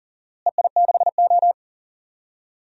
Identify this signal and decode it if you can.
Morse code, heard as a keyed tone, about 39 wpm, 700 Hz. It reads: EI6O
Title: DXCC-100 Call Signs - 4 Characters - Repeated 2x - 40wpm